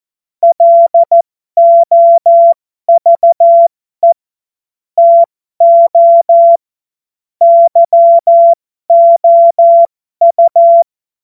Morse code, heard as a keyed tone, about 14 wpm, spaced as for 12 wpm, 680 Hz. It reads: LOVE TO YOU
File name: Morsekurs / Witze